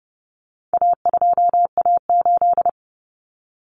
Morse code, heard as a keyed tone, about 30 wpm, 700 Hz. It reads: A2A8
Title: Common Cut Numbers - 4-Digit 30wpm